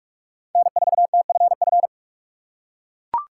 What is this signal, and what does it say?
Morse code, heard as a keyed tone, about 44 wpm, 700 Hz. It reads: N4TFF
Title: USA Call Signs - 5 Characters - 3x Character Spacing - 45wpm